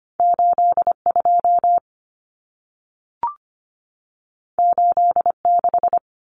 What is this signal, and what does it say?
Morse code, heard as a keyed tone, about 25 wpm, 700 Hz. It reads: 82 86
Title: Numbers - 2-Digit 25wpm